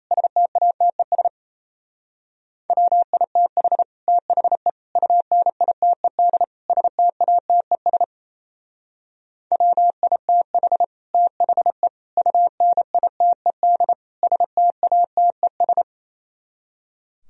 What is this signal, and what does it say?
Morse code, heard as a keyed tone, about 32 wpm, 700 Hz. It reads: STATES WITH THE UNITED STATES WITH THE UNITED STATES
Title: Speed-Racing - Sets of 4 Words 28wpm